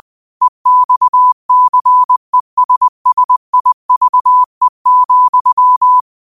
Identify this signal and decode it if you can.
Morse code, heard as a keyed tone, about 20 wpm, 1000 Hz. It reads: EXCESSIVE,
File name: qotdfiles / 20wpm